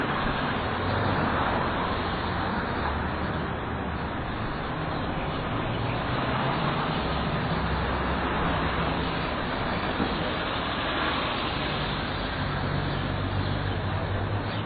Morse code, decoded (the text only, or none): none